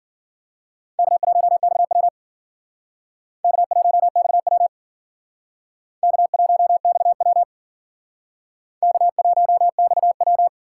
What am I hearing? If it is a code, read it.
Morse code, heard as a keyed tone, about 50 wpm, 700 Hz. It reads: K1XW K1XW K1XW K1XW
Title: Speed Racing - USA 1x2 Call Signs 40wpm